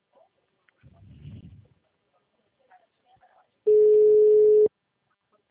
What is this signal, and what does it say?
Ringback tone (Europe, 425 Hz).